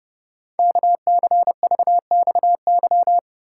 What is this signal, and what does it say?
Morse code, heard as a keyed tone, about 30 wpm, 700 Hz. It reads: KCVXY
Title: Code Groups - 5 Letters - 30wpm